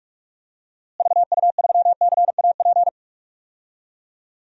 Morse code, heard as a keyed tone, about 46 wpm, 700 Hz. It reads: VU3CAP
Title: DXCC-100 Call Signs - 6 Characters - Repeated 3x - 45wpm